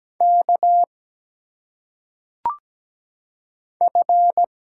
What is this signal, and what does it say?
Morse code, heard as a keyed tone, about 17 wpm, 700 Hz. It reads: K F
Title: Single Letters - Phonetic Pronunciation 17wpm